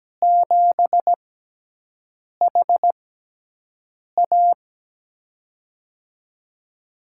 Morse code, heard as a keyed tone, about 17 wpm, spaced as for 6 wpm, 700 Hz. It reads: 7HA
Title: USA Call Signs - 5 Characters - 6x Character Spacing - 17wpm